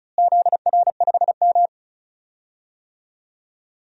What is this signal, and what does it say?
Morse code, heard as a keyed tone, about 36 wpm, 700 Hz. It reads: ZR5M
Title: DXCC-100 Call Signs - 4 Characters - Repeated 3x - 35wpm